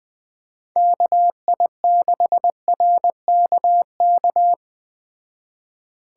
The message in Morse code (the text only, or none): KI6RKK